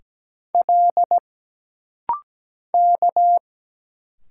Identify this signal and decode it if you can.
Morse code, heard as a keyed tone, about 17 wpm, 700 Hz. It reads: L K
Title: Single Letters - Rapid-Fire 17wpm